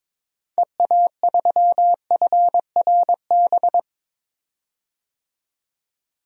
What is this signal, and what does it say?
Morse code, heard as a keyed tone, about 22 wpm, 700 Hz. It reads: EA3FRB
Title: DXCC-100 Call Signs - 6 Characters - 22wpm